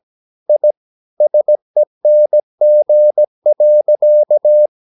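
Morse code, heard as a keyed tone, about 17 wpm, 600 Hz. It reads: I SENG.
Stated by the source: Slow (QRS) Morse practice at 17 wpm, from CW med Gnister.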